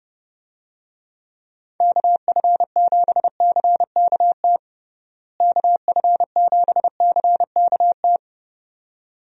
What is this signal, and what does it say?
Morse code, heard as a keyed tone, about 30 wpm, 700 Hz. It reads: KF7CKT KF7CKT
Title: USA Call Signs - 6 Characters - Repeated 2x - 30wpm